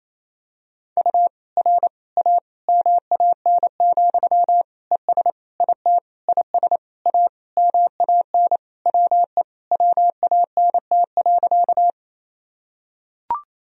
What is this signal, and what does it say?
Morse code, heard as a keyed tone, about 28 wpm, 700 Hz. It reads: U R A MAN, ES IT IS A MAN WE WANT.
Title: Sentences from Top 200 Words and Abbreviations 28wpm